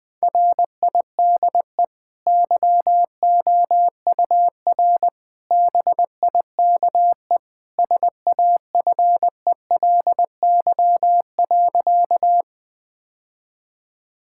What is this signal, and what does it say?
Morse code, heard as a keyed tone, about 21 wpm, 700 Hz. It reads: RIDE YOUR BIKE SAFELY.
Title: Easy Sentences with Hints - 20wpm